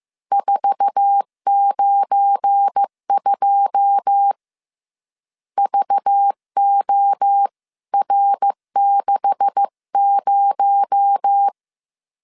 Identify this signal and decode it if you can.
Morse code, heard as a keyed tone, about 15 wpm, 800 Hz. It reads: 492 VOR60
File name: CW / DET / 1 9 1500C S A 14wpm 3222